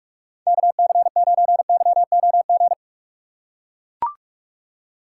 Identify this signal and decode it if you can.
Morse code, heard as a keyed tone, about 42 wpm, 700 Hz. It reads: KC9YOG
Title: USA Call Signs - 6 Characters - 3x Character Spacing - 45wpm